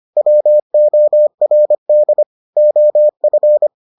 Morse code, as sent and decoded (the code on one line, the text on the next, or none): .-- --- .-. -.. / --- ..-.
WORD OF